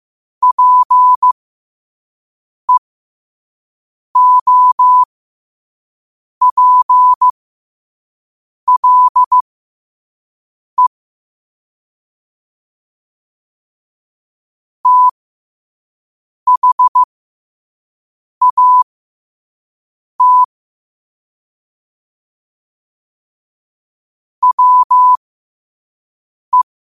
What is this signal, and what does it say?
Morse code, heard as a keyed tone, about 15 wpm, spaced as for 5 wpm, 1000 Hz. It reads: PEOPLE THAT WE
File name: qotdfiles / 5wpm